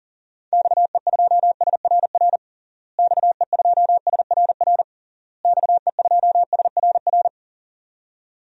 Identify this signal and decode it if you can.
Morse code, heard as a keyed tone, about 40 wpm, 700 Hz. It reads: XE2SRR XE2SRR XE2SRR
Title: DXCC-100 Call Signs - 6 Characters - Repeated 3x - 40wpm